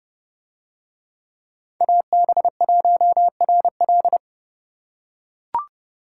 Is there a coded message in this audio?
Yes: Morse code from 1.8 to 4.2 s, heard as a keyed tone, about 30 wpm, 700 Hz. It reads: AB1RL